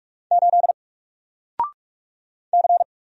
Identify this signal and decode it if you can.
Morse code, heard as a keyed tone, about 44 wpm, 700 Hz. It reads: 8 C
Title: Single Character (Letter:Number:Punctuation:Prosign) 45wpm